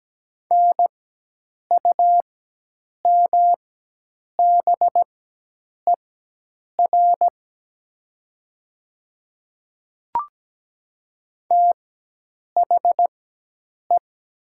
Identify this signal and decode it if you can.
Morse code, heard as a keyed tone, about 17 wpm, spaced as for 8 wpm, 700 Hz. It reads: NUMBER THE
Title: 4x Character Spacing - Top 100 Words - 17wpm